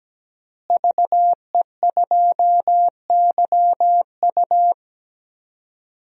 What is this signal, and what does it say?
Morse code, heard as a keyed tone, about 17 wpm, 700 Hz. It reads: VE2YU
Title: DXCC-100 Call Signs - 5 Characters - Repeated 2x - 17wpm